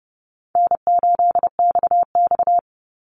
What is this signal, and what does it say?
Morse code, heard as a keyed tone, about 30 wpm, 700 Hz. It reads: N8XX
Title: Call Signs - 30wpm